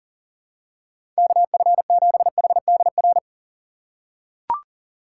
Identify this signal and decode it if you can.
Morse code, heard as a keyed tone, about 40 wpm, 700 Hz. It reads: KF7HDR
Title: USA Call Signs - 2x Character Spacing - 40wpm